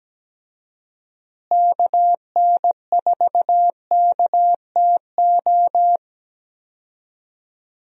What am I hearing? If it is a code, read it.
Morse code, heard as a keyed tone, about 17 wpm, 700 Hz. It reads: KN4KTO
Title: USA Call Signs - 6 Characters - Repeated 3x - 17wpm